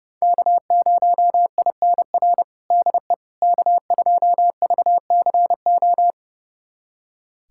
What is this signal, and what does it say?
Morse code, heard as a keyed tone, about 32 wpm, 700 Hz. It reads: K0INR DE K2VCO
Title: USA Callsign DE Callsign - 5 Characters - 30wpm